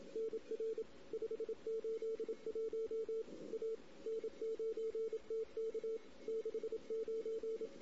Morse code, heard as a keyed tone, about 27 wpm, 440 Hz. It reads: NR 581 A N9TK 69
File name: CW / 2008-ss-cw-k5tr-op-n5rz -16:00